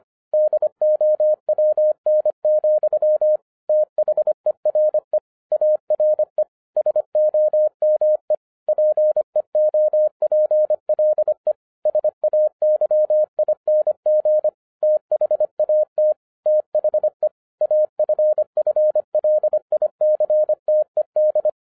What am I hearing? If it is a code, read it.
Morse code, heard as a keyed tone, about 25 wpm, 610 Hz. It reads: DOWN, THERE ARE SOME PEOPLE SAYING THAT THE AFFLICTED
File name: 2011-07-mp3 / n39 THE 3.11 2 25wpm